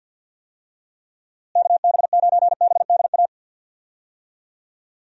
Morse code, heard as a keyed tone, about 52 wpm, 700 Hz. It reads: KB9BDA
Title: USA Call Signs - 6 Characters - Repeated 2x - 50wpm